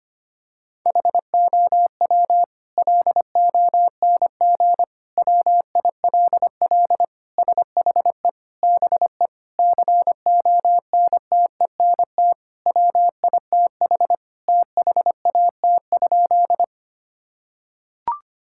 Morse code, heard as a keyed tone, about 25 wpm, 700 Hz. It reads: HOW LONG WILL SHE BE CONTENT WITH THAT?
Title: Sentences from Top 1000 Words - Review Part 6 - 25wpm